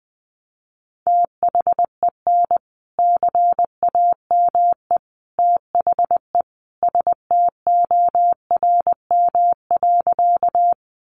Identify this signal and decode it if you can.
Morse code, heard as a keyed tone, about 19 wpm, 700 Hz. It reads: THEN CAME THE STORM.
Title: Sentences from Top 1500 Words 20wpm